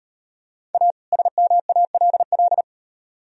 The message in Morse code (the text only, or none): A SMALL